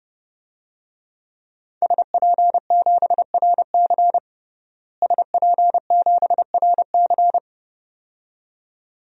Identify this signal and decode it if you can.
Morse code, heard as a keyed tone, about 30 wpm, 700 Hz. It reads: SP7RC SP7RC
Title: DXCC-100 Call Signs - 5 Characters - Repeated 2x - 30wpm